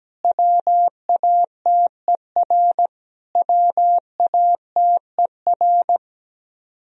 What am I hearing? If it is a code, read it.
Morse code, heard as a keyed tone, about 17 wpm, 700 Hz. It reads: WATER WATER
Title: Top 100 Words Repeated 17wpm